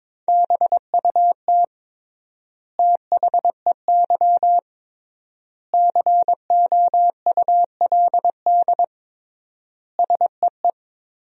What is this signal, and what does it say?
Morse code, heard as a keyed tone, about 22 wpm, 700 Hz. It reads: BUT THEY COULD SEE
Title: 3x Word-Spacing - Sentences from Top 500 Words - 22wpm